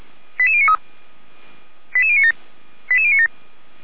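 Five-tone selective call (ZVEI), heard as quick runs of five tones, tones of about 70 ms. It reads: 88093 88097 88097
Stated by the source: ZVEI five-tone selective call 88057 received by SDR.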